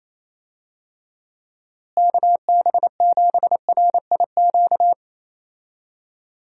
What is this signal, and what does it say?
Morse code, heard as a keyed tone, about 28 wpm, 700 Hz. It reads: KB7RIQ
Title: USA Call Signs - 6 Characters - Repeated 3x - 28wpm